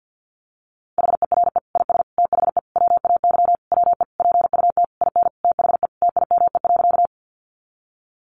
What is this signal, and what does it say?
Morse code, heard as a keyed tone, about 48 wpm, 700 Hz. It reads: HERE IS THE WAY WE PUT IN THE TIME.
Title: Sentences from Top 200 Words 50wpm